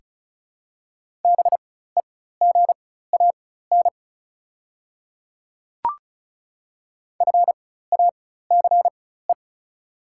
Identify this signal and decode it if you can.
Morse code, heard as a keyed tone, about 36 wpm, spaced as for 16 wpm, 690 Hz. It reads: BEGAN FACE
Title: 4x Character Spacing - Top 300 Words - 35wpm